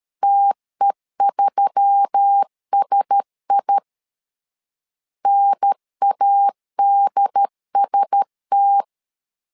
Morse code, heard as a keyed tone, about 13 wpm, 800 Hz. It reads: TE3SI NADST
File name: LEZIONI / 5 LBVC / 5 LB 12wpm 1906